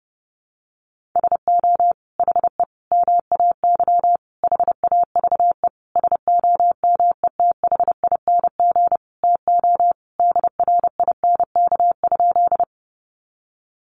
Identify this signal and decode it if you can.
Morse code, heard as a keyed tone, about 30 wpm, 700 Hz. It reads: SO HE MAY HAVE SOMETHING TO DRINK?
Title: Sentences from Top 1500 Words 30wpm